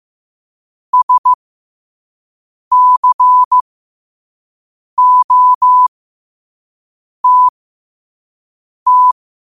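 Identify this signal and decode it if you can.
Morse code, heard as a keyed tone, about 15 wpm, spaced as for 5 wpm, 1000 Hz. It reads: SCOTT
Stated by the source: Morse code sent at 5 wpm.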